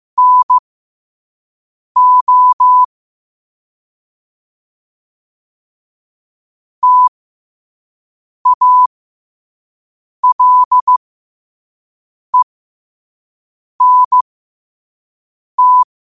Morse code, heard as a keyed tone, about 15 wpm, spaced as for 5 wpm, 1000 Hz. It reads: NO TALENT
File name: qotdfiles / 5wpm